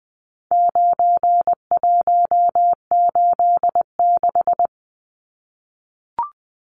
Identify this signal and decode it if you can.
Morse code, heard as a keyed tone, about 20 wpm, 700 Hz. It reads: 9186
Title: Numbers - 4-Digit 20wpm